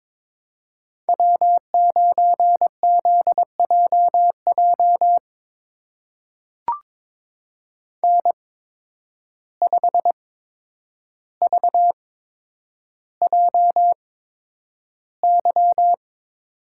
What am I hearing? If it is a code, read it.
Morse code, heard as a keyed tone, about 22 wpm, spaced as for 9 wpm, 700 Hz. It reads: W9ZJJ N5VJY
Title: USA Call Signs - 5 Characters - 8x Character Spacing - 22wpm